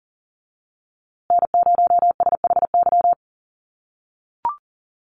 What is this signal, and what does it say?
Morse code, heard as a keyed tone, about 40 wpm, 700 Hz. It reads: N0SHY
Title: USA Call Signs Repeated 2x 40wpm